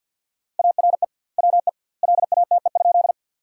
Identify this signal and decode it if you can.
Morse code, heard as a keyed tone, about 50 wpm, 700 Hz. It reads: ARE WE LATE?
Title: 2x Word-Spacing - Sentences from Top 300 Words - 50wpm